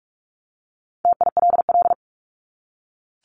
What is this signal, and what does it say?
Morse code, heard as a keyed tone, about 44 wpm, 700 Hz. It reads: TILL
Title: Top 1751-2000 Words 45wpm